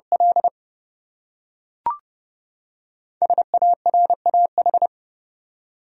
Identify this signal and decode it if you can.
Morse code, heard as a keyed tone, about 32 wpm, 700 Hz. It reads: L SARAH
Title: USA Common Names - Max 5 Letters - 30wpm